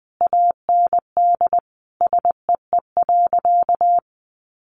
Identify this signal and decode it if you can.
Morse code, heard as a keyed tone, about 20 wpm, 700 Hz. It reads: AND SEE.
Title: Sentences from Top 100 Words 20wpm